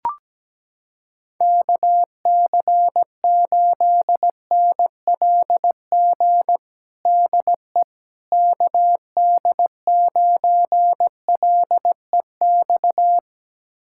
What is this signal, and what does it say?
Morse code, heard as a keyed tone, about 17 wpm, 700 Hz. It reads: KC8NLG DE KD9LEX
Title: USA Callsign DE Callsign - 6 Characters - 17wpm